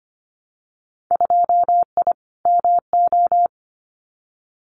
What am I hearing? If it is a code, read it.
Morse code, heard as a keyed tone, about 25 wpm, 700 Hz. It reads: 2I MO
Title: Winter Field Day Exchanges 25wpm